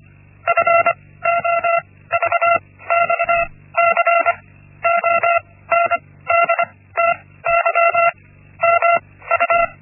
Morse code, heard as a keyed tone, about 25 wpm, 680 Hz. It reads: F O V X C O N D T Y M U
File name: Linksmakalnis-1